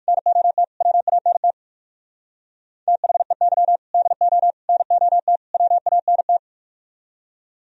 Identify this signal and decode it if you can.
Morse code, heard as a keyed tone, about 59 wpm, 700 Hz. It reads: NOT WANT THEY DO NOT WANT